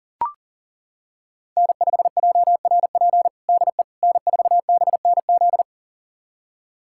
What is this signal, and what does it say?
Morse code, heard as a keyed tone, about 40 wpm, 700 Hz. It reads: N5JRP DE N4BNZ
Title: USA Callsign DE Callsign - 5 Characters - 40wpm